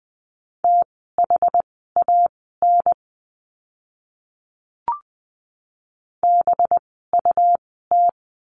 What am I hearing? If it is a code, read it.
Morse code, heard as a keyed tone, about 20 wpm, spaced as for 15 wpm, 700 Hz. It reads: THAN BUT
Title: 185 Lesson 30f - Letter b - Words 20wpm